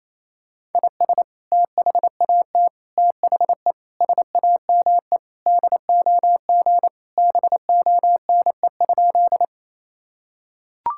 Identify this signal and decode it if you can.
Morse code, heard as a keyed tone, about 28 wpm, 700 Hz. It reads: IS THAT THE SAME DOG BONE?
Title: Sentences from Top 1600 Words 28wpm